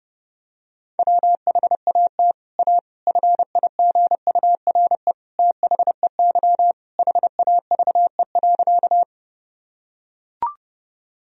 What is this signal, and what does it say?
Morse code, heard as a keyed tone, about 30 wpm, 700 Hz. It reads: WHAT A FIGURE THEY HAVE.
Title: Sentences from Top 1000 Words - Review Part 2 - 30wpm